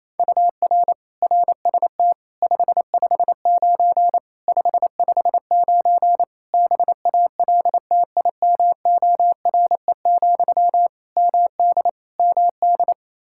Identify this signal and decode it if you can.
Morse code, heard as a keyed tone, about 28 wpm, 700 Hz. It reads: UR RST 559 559 BALTIMORE, MD MD
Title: Practice QSO - Exchange 1 - Beginning - 28wpm